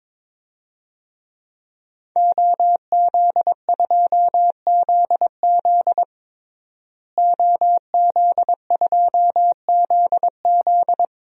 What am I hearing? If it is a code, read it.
Morse code, heard as a keyed tone, about 22 wpm, 700 Hz. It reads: OZ2ZZ OZ2ZZ
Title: DXCC-100 Call Signs - 5 Characters - Repeated 2x - 22wpm